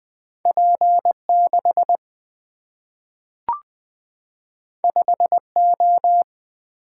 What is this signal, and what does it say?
Morse code, heard as a keyed tone, about 20 wpm, 700 Hz. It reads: P6 5O
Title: Two Letter-Number Combinations - 20wpm